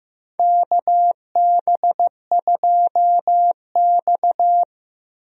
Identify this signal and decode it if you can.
Morse code, heard as a keyed tone, about 16 wpm, 700 Hz. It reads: KB2X
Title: USA Call Signs - 4 Characters - Repeated 3x - 15wpm